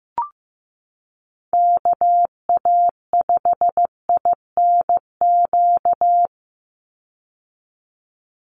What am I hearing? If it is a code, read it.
Morse code, heard as a keyed tone, about 15 wpm, 700 Hz. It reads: KA5INQ
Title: USA Call Signs 15wpm